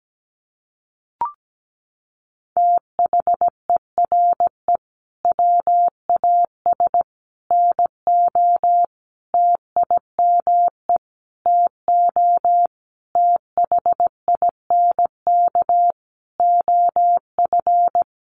Morse code, heard as a keyed tone, about 17 wpm, 700 Hz. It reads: THERE WAS NO TIME TO THINK OF
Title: Sentences from Top 400 Words 17wpm